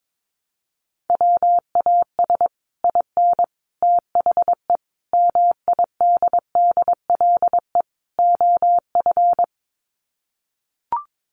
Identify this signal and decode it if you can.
Morse code, heard as a keyed tone, about 22 wpm, 700 Hz. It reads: WAS IN THE MIDDLE OF
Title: Sets of 5 Words Encore 22wpm